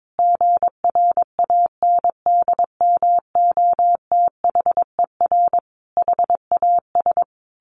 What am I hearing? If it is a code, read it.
Morse code, heard as a keyed tone, about 22 wpm, 700 Hz. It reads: GRANDMOTHER HAS
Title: Sentences from Top 1200 Words 22wpm